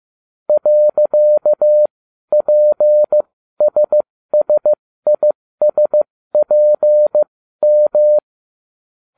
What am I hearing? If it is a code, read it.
Morse code, heard as a keyed tone, about 15 wpm, spaced as for 12 wpm, 600 Hz. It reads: .PSSISPM